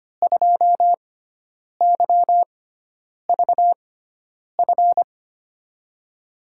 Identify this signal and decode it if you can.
Morse code, heard as a keyed tone, about 25 wpm, spaced as for 9 wpm, 700 Hz. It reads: 2YVF
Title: USA Call Signs - 6x Character Spacing - 25wpm